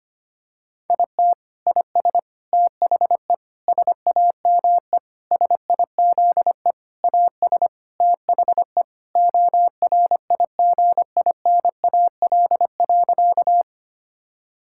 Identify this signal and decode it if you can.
Morse code, heard as a keyed tone, about 25 wpm, 700 Hz. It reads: IT IS THE SAME SIZE AS THE ORIGINAL.
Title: Sentences from Top 1000 Words - Review Part 6 - 25wpm